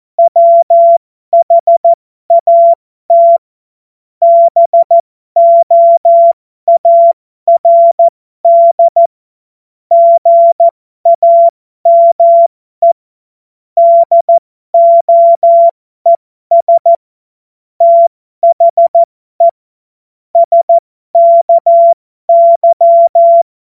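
Morse code, heard as a keyed tone, about 14 wpm, spaced as for 12 wpm, 680 Hz. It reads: WHAT BOARD GAME DOES THE SKY
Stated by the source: Morse code joke.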